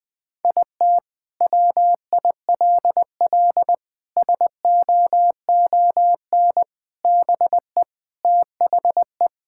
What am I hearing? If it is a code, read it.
Morse code, heard as a keyed tone, about 20 wpm, 700 Hz. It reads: IT WILL SOON BE THE
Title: Sentences from Top 1000 Words - Review Part 5 - 20wpm